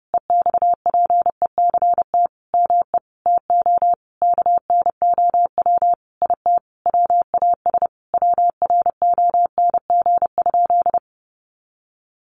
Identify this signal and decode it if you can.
Morse code, heard as a keyed tone, about 30 wpm, 700 Hz. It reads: EXPECT ME TO KNOW IT WAS WRONG?
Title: Sentences from Top 1100 Words 30wpm